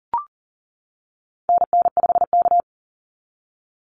Morse code, heard as a keyed tone, about 40 wpm, 700 Hz. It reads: NN5K
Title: NA SOTA Activator Callsigns 40wpm